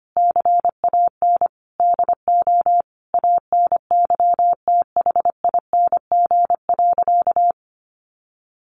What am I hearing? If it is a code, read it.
Morse code, heard as a keyed tone, about 25 wpm, 700 Hz. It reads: CAN DO ANYTHING.
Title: Sentences from Top 1100 Words 25wpm